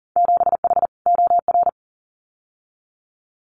Morse code, heard as a keyed tone, about 40 wpm, 700 Hz. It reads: 7H OR
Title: Winter Field Day Exchanges Encore 40wpm